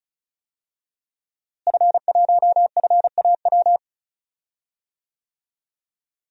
Morse code, heard as a keyed tone, about 34 wpm, 700 Hz. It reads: F1FAW
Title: DXCC-100 Call Signs - 5 Characters - Repeated 3x - 35wpm